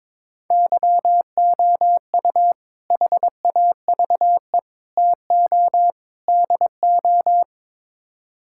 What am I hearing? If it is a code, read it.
Morse code, heard as a keyed tone, about 22 wpm, 700 Hz. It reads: YOU HAVE TO DO
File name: practice / 22wpm / Build A Phrase - 5-Word Phrases - 22wpm